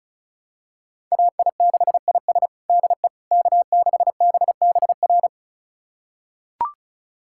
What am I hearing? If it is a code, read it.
Morse code, heard as a keyed tone, about 34 wpm, 690 Hz. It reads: AI6IS DE K6BBR
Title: USA Callsign DE Callsign - 5 Characters - 35wpm